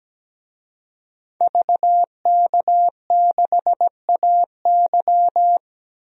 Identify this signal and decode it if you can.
Morse code, heard as a keyed tone, about 17 wpm, 700 Hz. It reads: VK6AY
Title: DXCC-100 Call Signs - 5 Characters - 17wpm